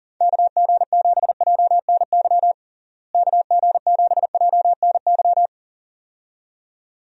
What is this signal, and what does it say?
Morse code, heard as a keyed tone, about 40 wpm, 700 Hz. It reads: KG7JNY KG7JNY
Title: USA Call Signs - 6 Characters - Repeated 2x - 40wpm